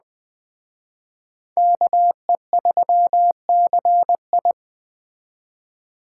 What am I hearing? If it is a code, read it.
Morse code, heard as a keyed tone, about 20 wpm, 700 Hz. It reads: KE3CI